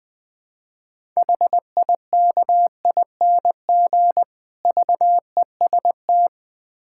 Morse code, heard as a keyed tone, about 22 wpm, 700 Hz. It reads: HIKING VEST
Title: Difficult Words - Speed-Racing with Hints - 20wpm